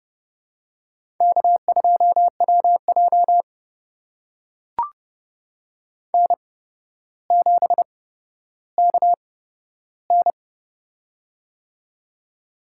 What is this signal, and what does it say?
Morse code, heard as a keyed tone, about 30 wpm, spaced as for 13 wpm, 700 Hz. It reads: K2WJ N7KN
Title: USA Call Signs - 4 Characters - 8x Character Spacing - 30wpm